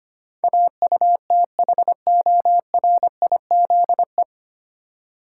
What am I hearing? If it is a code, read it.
Morse code, heard as a keyed tone, about 25 wpm, 700 Hz. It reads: AUTHORIZE
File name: practice / 25wpm / 9 Letter Words - First 3 Letters Given - 25wpm